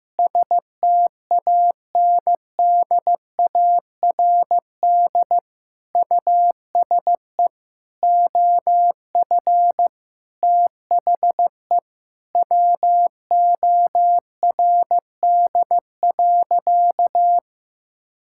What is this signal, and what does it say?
Morse code, heard as a keyed tone, about 15 wpm, 700 Hz. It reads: STANDARD USE OF THE WORD.
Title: Sentences from Top 1000 Words - Review Part 6 - 15wpm